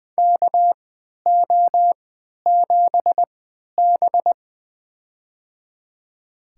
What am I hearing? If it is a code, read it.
Morse code, heard as a keyed tone, about 20 wpm, spaced as for 11 wpm, 700 Hz. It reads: KO7B
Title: USA Call Signs - 3x Character Spacing - 20wpm